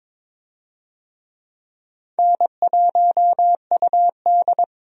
Morse code, heard as a keyed tone, about 22 wpm, 700 Hz. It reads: N1UD
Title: USA Call Signs - 4 Characters - 22wpm